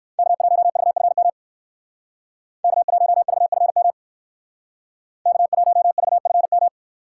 Morse code, heard as a keyed tone, about 58 wpm, 700 Hz. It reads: K1VFM K1VFM K1VFM